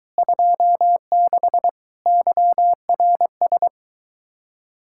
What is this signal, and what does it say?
Morse code, heard as a keyed tone, about 23 wpm, 700 Hz. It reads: 26 YRS